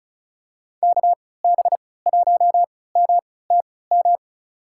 Morse code, heard as a keyed tone, about 36 wpm, spaced as for 20 wpm, 700 Hz. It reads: KB1MTM
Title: USA Call Signs - 6 Characters - 3x Character Spacing - 35wpm